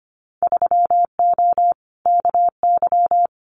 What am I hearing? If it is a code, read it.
Morse code, heard as a keyed tone, about 25 wpm, 700 Hz. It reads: 3O KY
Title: Winter Field Day Exchanges 25wpm